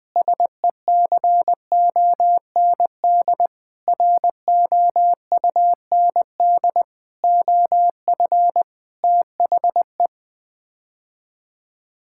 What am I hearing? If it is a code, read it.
Morse code, heard as a keyed tone, about 20 wpm, 700 Hz. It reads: SECOND ROUND OF THE
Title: Sets of 6 Words - 20wpm